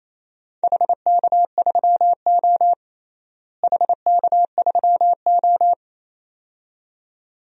Morse code, heard as a keyed tone, about 28 wpm, 700 Hz. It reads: HK3O HK3O
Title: DXCC-100 Call Signs - 4 Characters - Repeated 2x - 28wpm